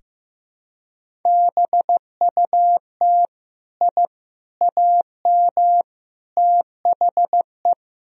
Morse code, heard as a keyed tone, about 15 wpm, 700 Hz. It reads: BUT I AM THE
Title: Sentences from Top 1000 Words - Review Part 4 - 15wpm